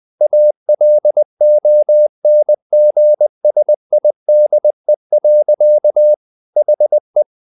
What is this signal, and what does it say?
Morse code, heard as a keyed tone, about 20 wpm, 600 Hz. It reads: ALONGSIDE. HE
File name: the old man and the sea part 2 20wpm